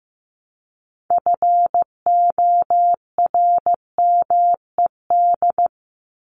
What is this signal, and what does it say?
Morse code, heard as a keyed tone, about 15 wpm, 700 Hz. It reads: FORMED
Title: ED Suffix - Top 100 Words with Base Words 15wpm